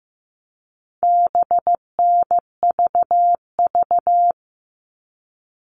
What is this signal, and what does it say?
Morse code, heard as a keyed tone, about 15 wpm, 700 Hz. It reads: BNVV
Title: All Cut Numbers - 4-Digit 15wpm